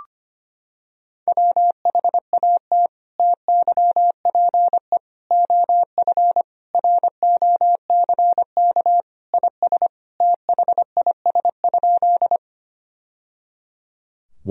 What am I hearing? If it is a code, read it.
Morse code, heard as a keyed tone, about 25 wpm, 700 Hz. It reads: WHAT TYPE OF ROCK IS THIS?
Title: Sentences from Top 1000 Words - Review Part 3 - 25wpm